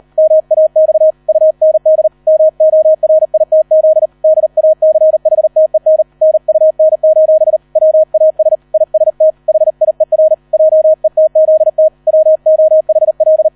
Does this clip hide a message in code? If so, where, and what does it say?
Yes: Morse code from 0.2 to 13.5 s, heard as a keyed tone, about 38 wpm, 630 Hz. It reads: MAX UND MORITZ DACHTEN NUN: WAS IST HIER JETZT WOHL